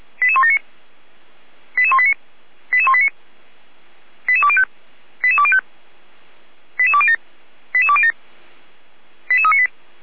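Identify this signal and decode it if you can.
Five-tone selective call (ZVEI), heard as quick runs of five tones, tones of about 70 ms. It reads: 88179 88189 88189 88205 88205 88207 88207 88398
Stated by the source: ZVEI five-tone selective call received on SDR, from feuerwehr-orlamuende.org fire brigade audio server.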